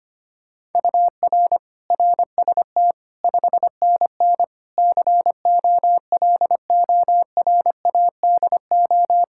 Morse code, heard as a keyed tone, about 25 wpm, 700 Hz. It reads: UR RST 5NN COLORADO
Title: Practice QSO - Exchange 1 - Beginning - 25wpm